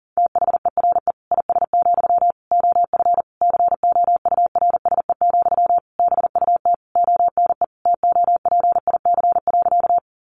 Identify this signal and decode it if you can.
Morse code, heard as a keyed tone, about 40 wpm, 700 Hz. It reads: THERE IS, OF COURSE, BUT ONE TOPIC.
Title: Sentences from Top 1200 Words 40wpm